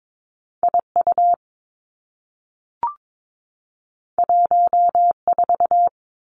Morse code, heard as a keyed tone, about 22 wpm, 700 Hz. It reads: I U 1 4
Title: Two Challenging Letter-Number Combinations 22wpm